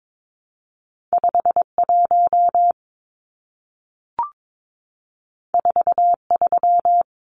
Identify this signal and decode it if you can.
Morse code, heard as a keyed tone, about 22 wpm, 700 Hz. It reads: 51 43
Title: Numbers - 2-Digit 22wpm